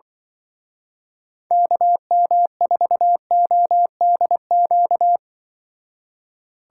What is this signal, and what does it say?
Morse code, heard as a keyed tone, about 24 wpm, 700 Hz. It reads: KM4ODQ
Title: Speed Racing - USA 2x3 Call Signs 20wpm